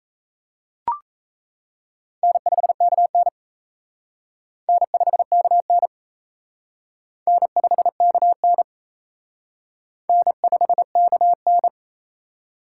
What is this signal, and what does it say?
Morse code, heard as a keyed tone, about 35 wpm, 700 Hz. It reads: N5KN N5KN N5KN N5KN
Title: Speed Racing - USA 1x2 Call Signs 28wpm